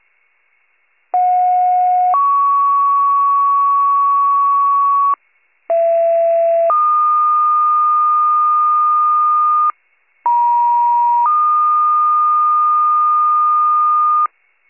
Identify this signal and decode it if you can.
Two-tone page, heard as a one-second tone, then a three-second tone, about 730 then 1120 Hz, about 670 then 1220 Hz, about 950 then 1220 Hz. It is heard from 1.1 to 14.3 s.